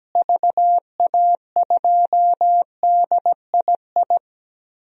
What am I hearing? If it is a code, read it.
Morse code, heard as a keyed tone, about 17 wpm, 700 Hz. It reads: VA2DII
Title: DXCC-100 Call Signs - 6 Characters - Repeated 2x - 17wpm